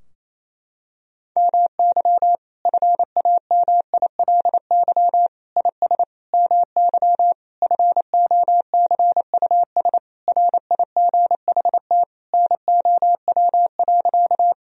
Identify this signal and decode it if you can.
Morse code, heard as a keyed tone, about 28 wpm, 700 Hz. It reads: MY FAMILY IS MY FOCUS RIGHT NOW.
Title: Sentences from Top 1000 Words - Review Part 4 - 28wpm